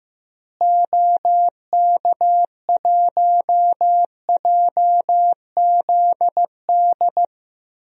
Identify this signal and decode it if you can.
Morse code, heard as a keyed tone, about 15 wpm, 700 Hz. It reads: OK1JZD
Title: DXCC-100 Call Signs - 6 Characters - Repeated 2x - 15wpm